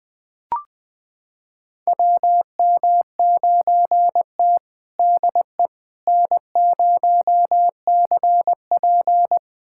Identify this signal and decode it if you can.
Morse code, heard as a keyed tone, about 21 wpm, 700 Hz. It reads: WM9T DE N0CP
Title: USA Callsign DE Callsign - 4 Characters - 20wpm